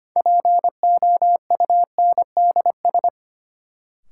Morse code, heard as a keyed tone, about 25 wpm, 700 Hz. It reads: POUNDS
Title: Sentences from Top 1000 Words - Review Part 2 - New Word Drill - 25wpm